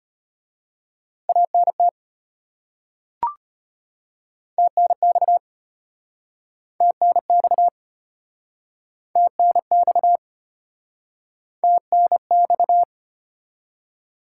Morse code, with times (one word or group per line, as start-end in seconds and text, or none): 1.3-1.9 ANT
4.6-5.4 TNX
6.8-7.7 TNX
9.2-10.2 TNX
11.6-12.8 TNX